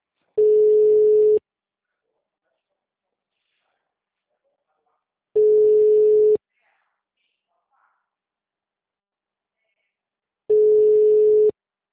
Ringback tone (Europe, 425 Hz).
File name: out- -2025- - - .20